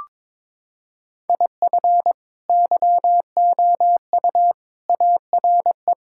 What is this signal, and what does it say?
Morse code, heard as a keyed tone, about 22 wpm, 700 Hz. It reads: IF YOU ARE
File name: practice / 22wpm / Sentences with 8-Word Phrases - 22wpm